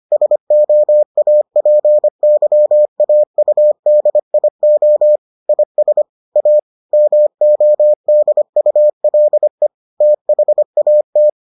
Morse code, heard as a keyed tone, about 25 wpm, 600 Hz. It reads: SOAPYAUDIO IS A MODULE THAT